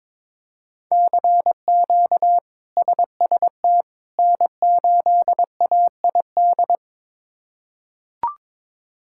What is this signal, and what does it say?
Morse code, heard as a keyed tone, about 22 wpm, 700 Hz. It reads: CQ SST N8AID
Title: K1USN SST QSO Practice - 22wpm